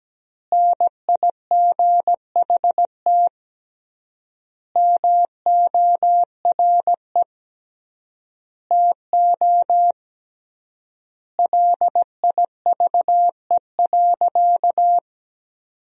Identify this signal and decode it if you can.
Morse code, heard as a keyed tone, about 17 wpm, 700 Hz. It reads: NIGHT MORE TO LIVE.
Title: 3x Word-Spacing - Sentences from Top 300 Words - 17wpm